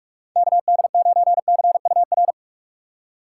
Morse code, heard as a keyed tone, about 61 wpm, spaced as for 53 wpm, 700 Hz. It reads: KD9CUR
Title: Speed Racing - USA 2x3 Call Signs 45wpm